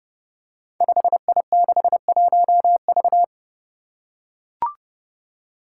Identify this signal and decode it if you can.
Morse code, heard as a keyed tone, about 30 wpm, 700 Hz. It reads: 5I61V
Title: Five Challenging Letter-Number Combinations 30wpm